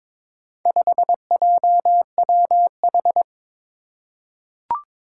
Morse code, heard as a keyed tone, about 22 wpm, 700 Hz. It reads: 5JWH
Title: Four Challenging Letter-Number Combinations 22wpm